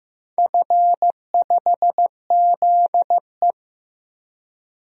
Morse code, heard as a keyed tone, about 15 wpm, 700 Hz. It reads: F5ZE
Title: DXCC-100 Call Signs - 4 Characters - Repeated 3x - 15wpm